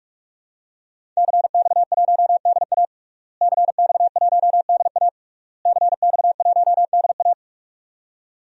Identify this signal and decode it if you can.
Morse code, heard as a keyed tone, about 46 wpm, 700 Hz. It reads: CX1DA CX1DA CX1DA
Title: DXCC-100 Call Signs - 5 Characters - Repeated 3x - 45wpm